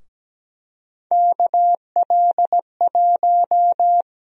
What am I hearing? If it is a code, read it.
Morse code, heard as a keyed tone, about 17 wpm, 700 Hz. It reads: KL1
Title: USA Call Sign Prefixes - 17wpm